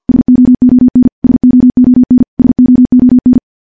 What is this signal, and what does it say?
Morse code, heard as a keyed tone, about 50 wpm, 250 Hz. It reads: SOON SOON SOON
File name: morse / morse-training-50WPM